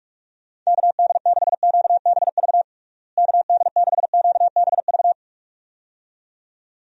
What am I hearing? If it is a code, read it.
Morse code, heard as a keyed tone, about 42 wpm, 700 Hz. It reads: KD6QBV KD6QBV
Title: USA Call Signs - 6 Characters - Repeated 2x - 45wpm